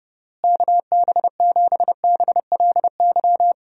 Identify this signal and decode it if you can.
Morse code, heard as a keyed tone, about 32 wpm, 700 Hz. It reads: KB7BLY
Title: USA Call Signs - 6 Characters - Repeated 3x - 30wpm